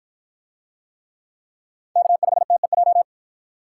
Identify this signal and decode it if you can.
Morse code, heard as a keyed tone, about 53 wpm, 700 Hz. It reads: K5TEJ